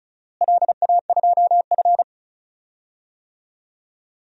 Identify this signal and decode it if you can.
Morse code, heard as a keyed tone, about 34 wpm, 700 Hz. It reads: LA2F
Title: DXCC-100 Call Signs - 4 Characters - Repeated 2x - 35wpm